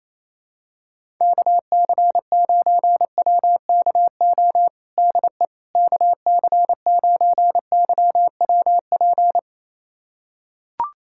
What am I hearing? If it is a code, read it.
Morse code, heard as a keyed tone, about 28 wpm, 700 Hz. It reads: KC9WKO DE KC9YWP
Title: USA Callsign DE Callsign 28wpm